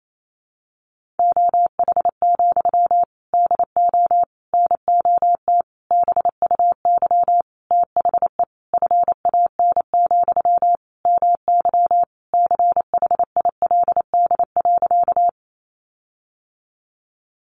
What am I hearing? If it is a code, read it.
Morse code, heard as a keyed tone, about 28 wpm, 700 Hz. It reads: OH, DO NOT BUY THE FAN, MY CHILD.
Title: Sentences from Top 1300 Words 28wpm